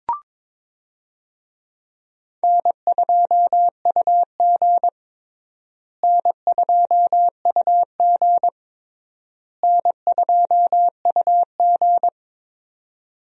Morse code, heard as a keyed tone, about 21 wpm, 700 Hz. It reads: N2UG N2UG N2UG
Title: USA Call Signs - 4 Characters - Repeated 3x - 22wpm